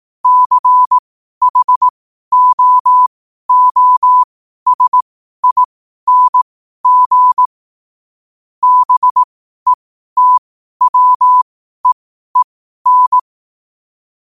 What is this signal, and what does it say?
Morse code, heard as a keyed tone, about 18 wpm, spaced as for 12 wpm, 1000 Hz. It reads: CHOOSING BETWEEN
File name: qotdfiles / 12wpm